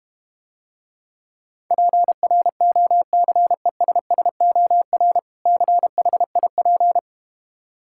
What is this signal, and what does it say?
Morse code, heard as a keyed tone, about 32 wpm, 700 Hz. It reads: PROCESSOR CHIP